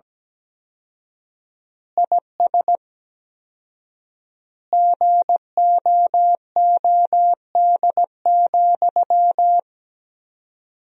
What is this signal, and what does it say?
Morse code, heard as a keyed tone, about 17 wpm, 700 Hz. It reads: IS GOOD,
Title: 4x Word-Spacing - Sentences from Top 200 Words - 17wpm